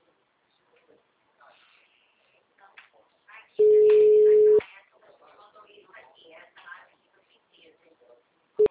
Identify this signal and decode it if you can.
Ringback tone (Europe, 425 Hz), with speech.
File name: out- -2021- - - .84818